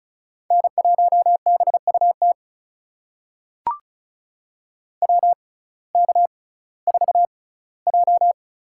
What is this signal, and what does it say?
Morse code, heard as a keyed tone, about 34 wpm, spaced as for 19 wpm, 700 Hz. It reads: N1BUT WK4J
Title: USA Call Signs - 6x Character Spacing - 35wpm